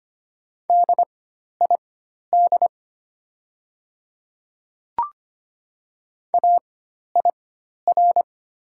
Morse code, heard as a keyed tone, about 25 wpm, spaced as for 12 wpm, 700 Hz. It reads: DID AIR
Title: 4x Character Spacing - Top 300 Words - 25wpm